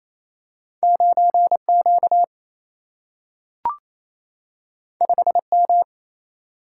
Morse code, heard as a keyed tone, about 28 wpm, 700 Hz. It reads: 9 Q 5 M
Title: Single-Letter Single-Number 28wpm